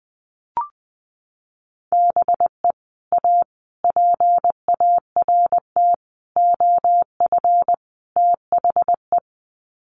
Morse code, heard as a keyed tone, about 19 wpm, 700 Hz. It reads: BE A PART OF THE
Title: Sets of 5 Words Encore 20wpm